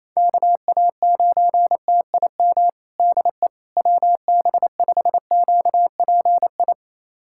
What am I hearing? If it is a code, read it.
Morse code, heard as a keyed tone, about 28 wpm, 700 Hz. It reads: KA9TIM DE WB5QPI
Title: USA Callsign DE Callsign - 6 Characters - 28wpm